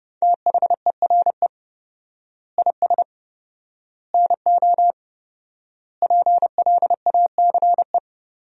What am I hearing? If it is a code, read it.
Morse code, heard as a keyed tone, about 30 wpm, 700 Hz. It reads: THERE IS NO PLACE
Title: 4x Word-Spacing - Sentences from Top 200 Words - 30wpm